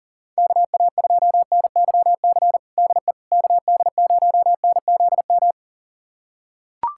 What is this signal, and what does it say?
Morse code, heard as a keyed tone, about 38 wpm, 700 Hz. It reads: KA2NYC DE KD0NZM
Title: USA Callsign DE Callsign - 6 Characters - 40wpm